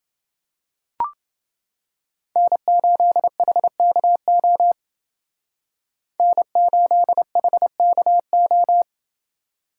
Morse code, heard as a keyed tone, about 29 wpm, 700 Hz. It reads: N8HKO N8HKO